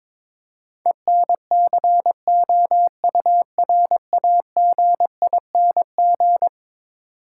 Morse code, heard as a keyed tone, about 22 wpm, 700 Hz. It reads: ENCOURAGING
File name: practice / 22wpm / Eleven Letter Words - 22wpm